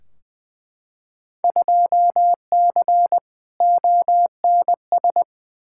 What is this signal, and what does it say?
Morse code, heard as a keyed tone, about 20 wpm, 700 Hz. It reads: 2C ONS
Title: Field Day Exchanges Encore - 20wpm